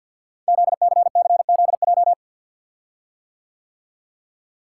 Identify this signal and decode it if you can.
Morse code, heard as a keyed tone, about 52 wpm, 700 Hz. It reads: ZCCZJ
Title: Code Groups - 5 Letters - 50wpm